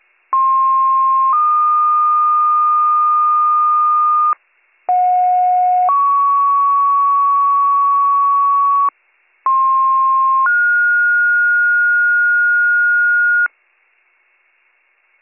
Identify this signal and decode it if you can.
Two-tone page, heard as a one-second tone, then a three-second tone, about 1060 then 1220 Hz, about 730 then 1120 Hz, about 1060 then 1430 Hz. It is heard from 0.3 to 13.5 s.